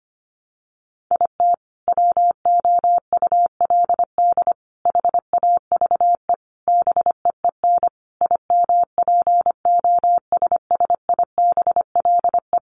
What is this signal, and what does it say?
Morse code, heard as a keyed tone, about 25 wpm, 700 Hz. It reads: IT WOULD HAVE BEEN IMPOSSIBLE
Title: Sentences from Top 1100 Words 25wpm